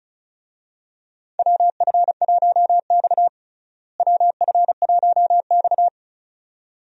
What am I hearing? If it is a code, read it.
Morse code, heard as a keyed tone, about 34 wpm, 700 Hz. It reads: WF1X WF1X
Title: USA Call Signs - 4 Characters - Repeated 2x - 35wpm